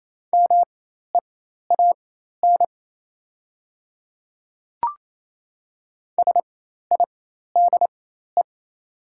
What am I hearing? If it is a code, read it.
Morse code, heard as a keyed tone, about 28 wpm, spaced as for 13 wpm, 700 Hz. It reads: MEAN SIDE
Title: 4x Character Spacing - Top 200 Words - 28wpm